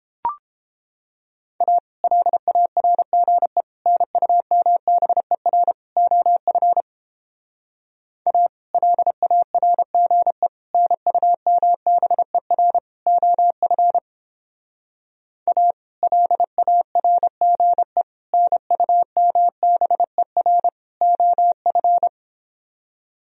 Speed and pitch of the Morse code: 29 wpm, 700 Hz

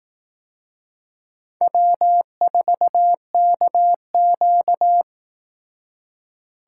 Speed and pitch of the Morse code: 20 wpm, 700 Hz

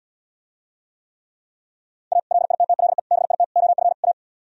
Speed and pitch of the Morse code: 77 wpm, 700 Hz

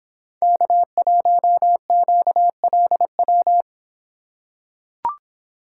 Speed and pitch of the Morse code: 24 wpm, 700 Hz

